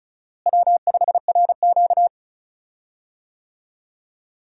Morse code, heard as a keyed tone, about 34 wpm, 700 Hz. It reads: W5RQ
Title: USA Call Signs - 4 Characters - Repeated 2x - 35wpm